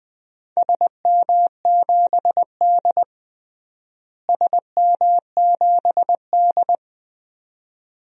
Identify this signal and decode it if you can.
Morse code, heard as a keyed tone, about 21 wpm, 700 Hz. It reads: SM7D SM7D
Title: DXCC-100 Call Signs - 4 Characters - Repeated 2x - 20wpm